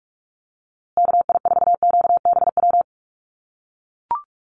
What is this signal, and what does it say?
Morse code, heard as a keyed tone, about 44 wpm, 700 Hz. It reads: KI4QBW
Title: USA Call Signs Repeated 2x 45wpm